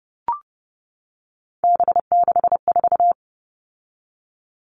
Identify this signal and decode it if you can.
Morse code, heard as a keyed tone, about 30 wpm, 700 Hz. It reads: B64
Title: All Cut Numbers - 3-Digit 30wpm